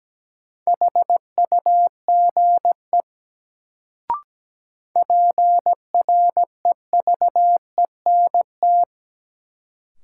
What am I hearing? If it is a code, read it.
Morse code, heard as a keyed tone, about 17 wpm, 700 Hz. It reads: HUGE PREVENT
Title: Sentences from Top 1100 Words - New Word Drill Alt - 17wpm